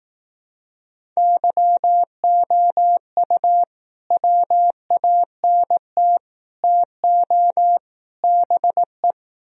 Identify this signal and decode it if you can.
Morse code, heard as a keyed tone, about 18 wpm, 700 Hz. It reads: YOU WANT TO BE
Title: Speed-Racing - Sets of 4 Words 15wpm